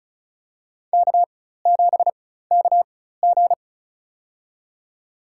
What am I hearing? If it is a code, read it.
Morse code, heard as a keyed tone, about 36 wpm, spaced as for 16 wpm, 700 Hz. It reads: K7KG
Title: USA Call Signs - 4x Character Spacing - 35wpm